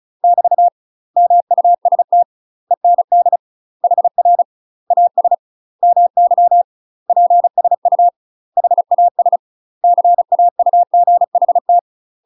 Morse code, heard as a keyed tone, about 36 wpm, 700 Hz. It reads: <BT> MUST END HR AS MY PSU HAS CAUGHT
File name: UK MASTER QSOs TRAINING / 35WPM / UK MASTER QSOs 0032